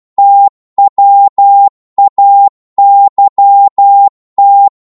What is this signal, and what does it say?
Morse code, heard as a keyed tone, about 12 wpm, 800 Hz. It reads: TWAYT
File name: Morse / ltr-12wpm